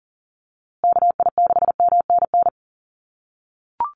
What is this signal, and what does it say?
Morse code, heard as a keyed tone, about 40 wpm, 700 Hz. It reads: KI6MNN
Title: USA Call Signs Repeated 2x 40wpm